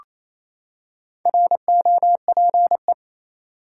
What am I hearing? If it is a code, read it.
Morse code, heard as a keyed tone, about 27 wpm, 700 Hz. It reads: ROPE
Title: Four Letter Words - 28wpm